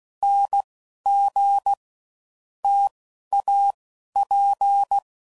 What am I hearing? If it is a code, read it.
Morse code, heard as a keyed tone, about 16 wpm, 800 Hz. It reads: N G T A P
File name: Combinations / Difficult Combos / T AOEI's